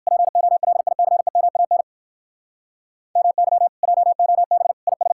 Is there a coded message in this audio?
Yes: Morse code from 0.1 to 5.1 s, heard as a keyed tone, about 57 wpm, 700 Hz. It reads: POLICEMAN MY JOB IS